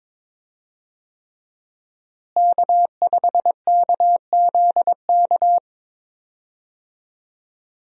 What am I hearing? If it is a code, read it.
Morse code, heard as a keyed tone, about 22 wpm, 700 Hz. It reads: K5KZK
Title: USA Call Signs - 5 Characters - Repeated 3x - 22wpm